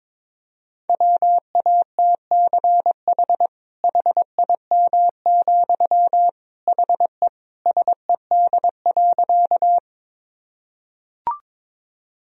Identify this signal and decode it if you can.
Morse code, heard as a keyed tone, about 22 wpm, 700 Hz. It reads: WATCH HIM, HE SED.
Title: Sentences from Top 300 Words and Abbreviations 22wpm